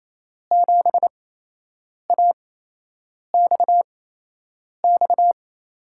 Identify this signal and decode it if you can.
Morse code, heard as a keyed tone, about 27 wpm, spaced as for 8 wpm, 700 Hz. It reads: 7AXX
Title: USA Call Signs - 6 Characters - 8x Character Spacing - 28wpm